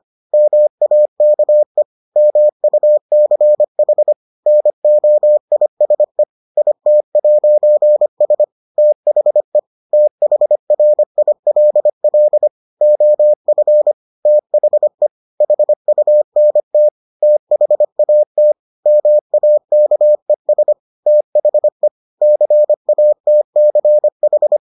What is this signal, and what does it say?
Morse code, heard as a keyed tone, about 25 wpm, 600 Hz. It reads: MAKE MUCH NOISE IT'S THE THRILL OF THE HUNT THAT MAKES THE CATCH